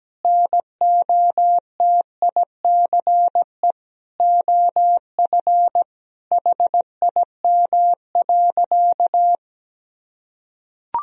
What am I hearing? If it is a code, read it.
Morse code, heard as a keyed tone, about 17 wpm, 700 Hz. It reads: NOTICE OF HIM.
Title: Sentences from Top 1000 Words - Review Part 4 - 17wpm